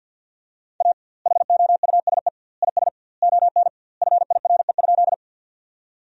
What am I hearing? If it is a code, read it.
Morse code, heard as a keyed tone, about 50 wpm, 700 Hz. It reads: A HOUSE IS ON FIRE?
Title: 2x Word-Spacing - Sentences from Top 400 Words - 50wpm